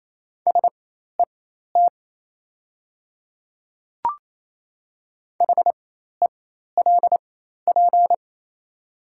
Morse code, heard as a keyed tone, about 28 wpm, spaced as for 13 wpm, 690 Hz. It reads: SET HELP
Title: 4x Character Spacing - Top 200 Words - 28wpm